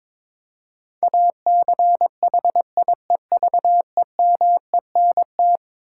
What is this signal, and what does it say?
Morse code, heard as a keyed tone, about 22 wpm, 700 Hz. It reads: ACHIEVEMENT
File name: practice / 22wpm / Eleven Letter Words - 22wpm